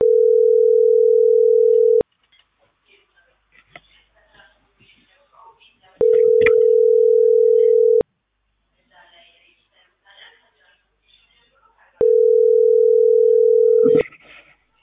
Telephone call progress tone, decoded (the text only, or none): ringback tone